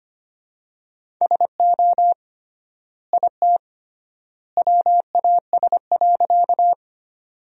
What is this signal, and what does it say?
Morse code, heard as a keyed tone, about 25 wpm, 700 Hz. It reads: SO IT WAS.
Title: 3x Word-Spacing - Sentences from Top 100 Words - 25wpm